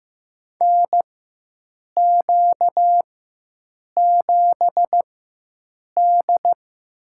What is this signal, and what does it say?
Morse code, heard as a keyed tone, about 15 wpm, spaced as for 7 wpm, 700 Hz. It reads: NQ7D
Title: USA Call Signs - 4 Characters - 4x Character Spacing - 15wpm